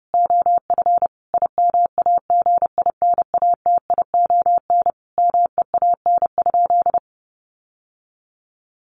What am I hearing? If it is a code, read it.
Morse code, heard as a keyed tone, about 30 wpm, 700 Hz. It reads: OF IMAGINATION MEAN?
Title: Sentences from Top 1300 Words 30wpm